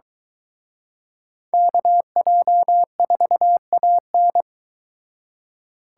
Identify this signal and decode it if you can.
Morse code, heard as a keyed tone, about 23 wpm, 700 Hz. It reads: KJ4AN